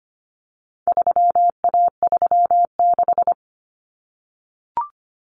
Morse code, heard as a keyed tone, about 25 wpm, 700 Hz. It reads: 3A36
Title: Common Cut Numbers - 4-Digit 25wpm